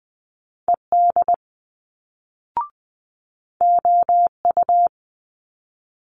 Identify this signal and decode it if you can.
Morse code, heard as a keyed tone, about 19 wpm, 700 Hz. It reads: ED OU
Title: Multi-letter Phonemes Quick Fire 20wpm